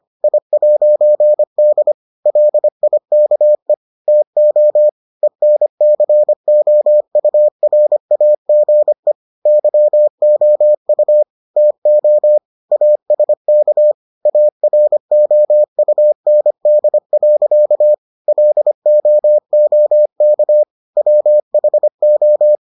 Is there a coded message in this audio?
Yes: Morse code from 0.2 to 22.6 s, heard as a keyed tone, about 25 wpm, 600 Hz. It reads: I'D LIKE TO ENCOURAGE YOU TO ASK AROUND. LOOK WHO